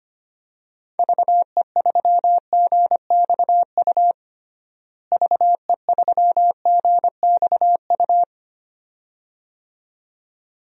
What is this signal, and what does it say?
Morse code, heard as a keyed tone, about 25 wpm, 700 Hz. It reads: VE3GXU VE3GXU
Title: DXCC-100 Call Signs - 6 Characters - Repeated 2x - 25wpm